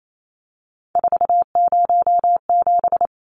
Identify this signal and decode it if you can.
Morse code, heard as a keyed tone, about 28 wpm, 700 Hz. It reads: 407
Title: Numbers - 3-Digit 28wpm